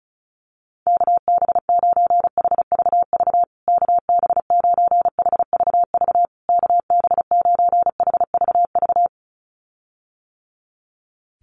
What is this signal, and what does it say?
Morse code, heard as a keyed tone, about 34 wpm, 700 Hz. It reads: KB9HVV KB9HVV KB9HVV
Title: USA Call Signs Repeated 3x 35wpm